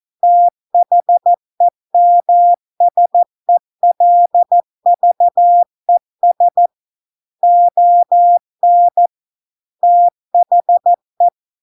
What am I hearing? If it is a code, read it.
Morse code, heard as a keyed tone, about 14 wpm, 700 Hz. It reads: THEMSELVES ON THE